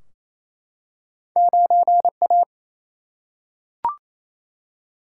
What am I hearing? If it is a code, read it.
Morse code, heard as a keyed tone, about 28 wpm, 700 Hz. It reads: 9A
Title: Single-Letter Single-Number 28wpm